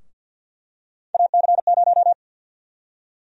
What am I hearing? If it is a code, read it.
Morse code, heard as a keyed tone, about 52 wpm, 700 Hz. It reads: AC0
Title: USA Call Sign Prefixes - Repeated 2x - 50wpm